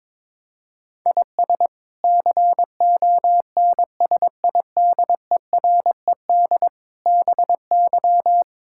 Morse code, heard as a keyed tone, about 22 wpm, 700 Hz. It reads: IS CONSIDERED BY
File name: practice / 22wpm / Sets of 7 Words - 22wpm